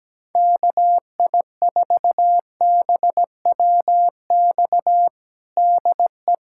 Morse code, heard as a keyed tone, about 17 wpm, 700 Hz. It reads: KI4BWX DE
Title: USA Callsign DE Callsign - 6 Characters - 17wpm